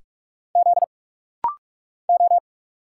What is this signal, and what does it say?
Morse code, heard as a keyed tone, about 46 wpm, 700 Hz. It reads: Z O
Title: Single Letter-Number Alt - Rapid-Fire 45wpm